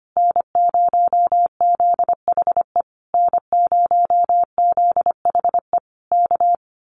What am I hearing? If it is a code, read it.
Morse code, heard as a keyed tone, about 25 wpm, 700 Hz. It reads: N0ZHE N0ZHE K
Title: Common QSO Phrases Encore 25wpm